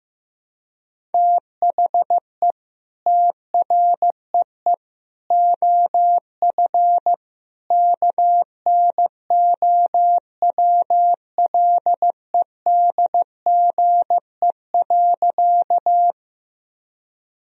Morse code, heard as a keyed tone, about 15 wpm, 700 Hz. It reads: THE TREE OF KNOWLEDGE.
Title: Sentences from Top 1000 Words - Review Part 3 - 15wpm